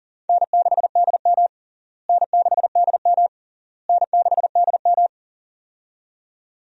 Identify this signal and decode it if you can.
Morse code, heard as a keyed tone, about 40 wpm, 700 Hz. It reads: N6DM N6DM N6DM
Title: USA Call Signs - 4 Characters - Repeated 3x - 40wpm